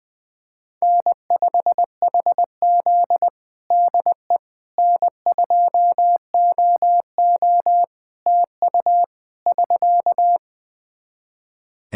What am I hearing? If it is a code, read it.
Morse code, heard as a keyed tone, about 21 wpm, 700 Hz. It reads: N5HZ DE N2OO TU <SK>
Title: Practice QSO - Exchange 4 - Ending - 20wpm